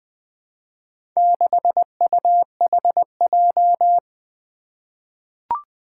Morse code, heard as a keyed tone, about 20 wpm, 700 Hz. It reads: 6UHJ
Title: Four Challenging Letter-Number Combinations 20wpm